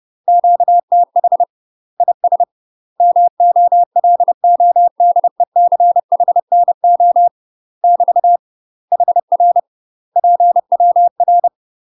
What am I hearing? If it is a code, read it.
Morse code, heard as a keyed tone, about 30 wpm, 700 Hz. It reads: QTH IS MOLODECHNO <BT> HR PWR
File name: UK MASTER QSOs TRAINING / 30WPM / UK MASTER QSOs 0049